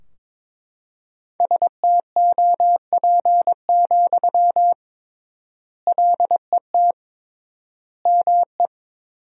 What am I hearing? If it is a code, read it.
Morse code, heard as a keyed tone, about 22 wpm, 700 Hz. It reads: STOP, LET ME
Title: 3x Word-Spacing - Sentences from Top 300 Words - 22wpm